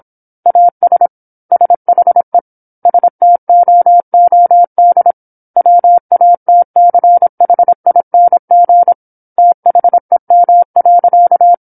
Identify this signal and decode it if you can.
Morse code, heard as a keyed tone, about 27 wpm, 700 Hz. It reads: AS SHE STOOD WATCHING THEM.
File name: 08-Chapter-08 26 WPM